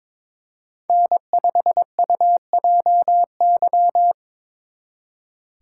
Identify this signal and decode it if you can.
Morse code, heard as a keyed tone, about 26 wpm, 700 Hz. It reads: N5UJY
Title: Speed Racing - USA 1x3 Call Signs 22wpm